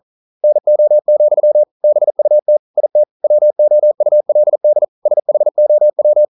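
Morse code, heard as a keyed tone, about 42 wpm, 600 Hz. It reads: NO, BUT IT WOULD SHOW
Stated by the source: Morse code practice at 41 wpm, from CW med Gnister.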